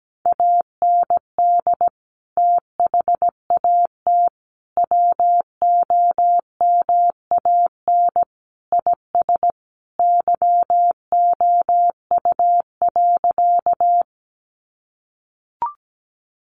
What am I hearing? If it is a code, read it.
Morse code, heard as a keyed tone, about 17 wpm, 700 Hz. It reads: AND THAT WOMAN IS YOU.
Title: Sentences from Top 1100 Words 17wpm